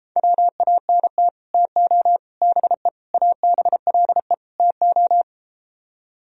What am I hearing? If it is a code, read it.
Morse code, heard as a keyed tone, about 33 wpm, 700 Hz. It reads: WANT TO BE ABLE TO